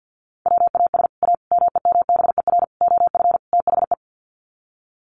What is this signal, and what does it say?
Morse code, heard as a keyed tone, about 50 wpm, 700 Hz. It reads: WAS A MEMBER OF THE
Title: Sets of 5 Words 50wpm